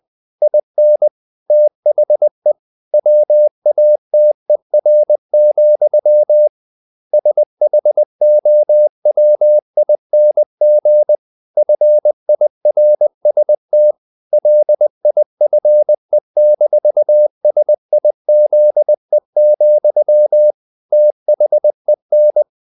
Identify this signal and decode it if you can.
Morse code, heard as a keyed tone, about 20 wpm, 600 Hz. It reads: IN THE WATER, SHOWING FIRST LIFE-SIZE, THEN
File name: the old man and the sea part 2 20wpm